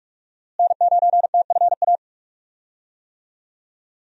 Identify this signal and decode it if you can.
Morse code, heard as a keyed tone, about 46 wpm, 700 Hz. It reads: N9TFA
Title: USA Call Signs - 5 Characters - Repeated 3x - 45wpm